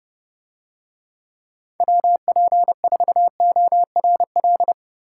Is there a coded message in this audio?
Yes: Morse code from 1.8 to 4.7 s, heard as a keyed tone, about 30 wpm, 700 Hz. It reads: WP4ORL